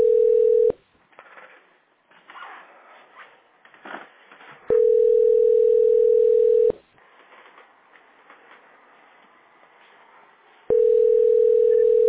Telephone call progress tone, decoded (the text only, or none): ringback tone